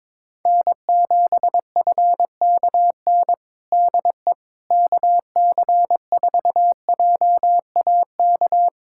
Morse code, heard as a keyed tone, about 22 wpm, 700 Hz. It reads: N7FKN DE KC4JAK
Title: Practice QSO - Exchange 4 - Ending - 22wpm